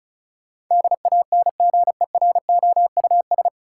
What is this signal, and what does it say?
Morse code, heard as a keyed tone, about 34 wpm, 700 Hz. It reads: DANGEROUS
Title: Sentences from Top 1200 Words - New Word Drill - 35wpm